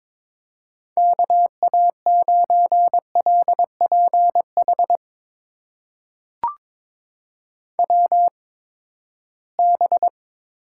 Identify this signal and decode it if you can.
Morse code, heard as a keyed tone, about 22 wpm, spaced as for 15 wpm, 700 Hz. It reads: KA9LPH WB
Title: USA Call Signs - 6 Characters - 8x Character Spacing - 22wpm